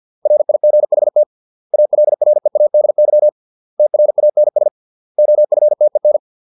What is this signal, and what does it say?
Morse code, heard as a keyed tone, about 50 wpm, 600 Hz. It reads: RIGHT ALREADY TRANS OFTEN